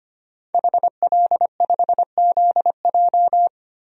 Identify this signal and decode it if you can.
Morse code, heard as a keyed tone, about 25 wpm, 700 Hz. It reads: HL5ZJ
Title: DXCC-100 Call Signs - 5 Characters - 25wpm